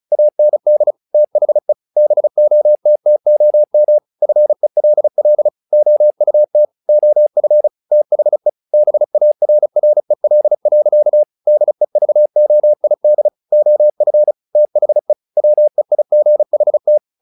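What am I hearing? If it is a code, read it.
Morse code, heard as a keyed tone, about 34 wpm, 600 Hz. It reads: AND THE BOTTOM FELL OUT OF THE BARREL. DEVOID OF THE WEIGHT
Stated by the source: Morse practice text "Tower Trouble" sent at 35 WPM, from the SKCC - Straight Key Century Club learning center audio files.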